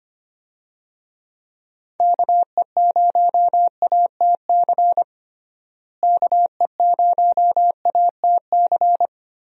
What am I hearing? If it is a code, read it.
Morse code, heard as a keyed tone, about 25 wpm, 700 Hz. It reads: KE0ATC KE0ATC
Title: USA Call Signs - 6 Characters - Repeated 3x - 25wpm